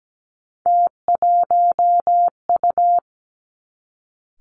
Morse code, heard as a keyed tone, about 17 wpm, 700 Hz. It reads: T1U
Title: Three Challenging Letter-Number Combinations 17wpm